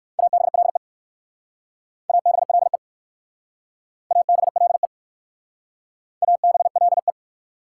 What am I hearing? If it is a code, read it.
Morse code, heard as a keyed tone, about 59 wpm, 690 Hz. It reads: ABLE ABLE ABLE ABLE